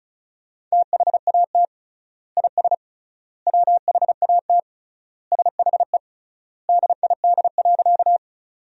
Morse code, heard as a keyed tone, about 36 wpm, 700 Hz. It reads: THAT IS WHAT SHE DID.
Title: 3x Word-Spacing - Sentences from Top 100 Words - 35wpm